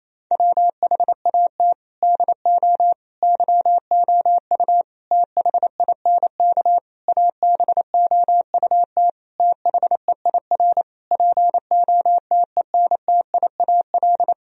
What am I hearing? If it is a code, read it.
Morse code, heard as a keyed tone, about 28 wpm, 700 Hz. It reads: WHAT DO YOU THINK ABOUT THEIR POTENTIAL
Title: Sentences from Top 1000 Words - Review Part 4 - 28wpm